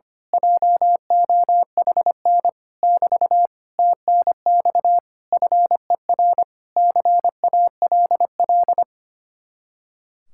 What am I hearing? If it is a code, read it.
Morse code, heard as a keyed tone, about 24 wpm, 700 Hz. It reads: JOHN <BT> TNX FER CALL
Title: Speed-Racing Common QSO Phrases Encore - 25wpm